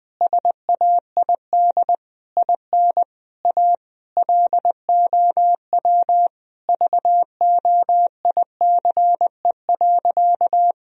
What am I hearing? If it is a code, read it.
Morse code, heard as a keyed tone, about 20 wpm, 700 Hz. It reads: SAID IN A LOW VOICE.
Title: Sentences from Top 1000 Words - Review Part 1 - 20wpm